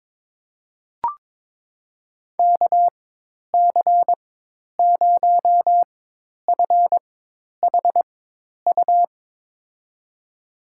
Morse code, heard as a keyed tone, about 22 wpm, spaced as for 10 wpm, 700 Hz. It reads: KC0FHU
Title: USA Call Signs - 6 Characters - 4x Character Spacing - 22wpm